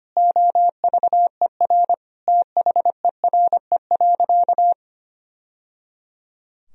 Morse code, heard as a keyed tone, about 25 wpm, 700 Hz. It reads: OVER THERE.
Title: Sentences from Top 1000 Words - Review Part 2 - 25wpm